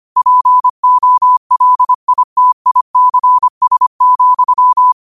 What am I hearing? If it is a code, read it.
Morse code, heard as a keyed tone, about 25 wpm, 1000 Hz. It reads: POLITICS,